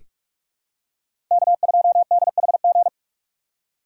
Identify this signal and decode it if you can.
Morse code, heard as a keyed tone, about 44 wpm, 700 Hz. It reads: K2DHG
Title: USA Call Signs - 5 Characters - 6x Character Spacing - 45wpm